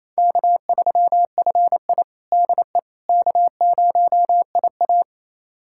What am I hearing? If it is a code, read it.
Morse code, heard as a keyed tone, about 28 wpm, 700 Hz. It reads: K3FI DE K0IA
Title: USA Callsign DE Callsign - 4 Characters - 28wpm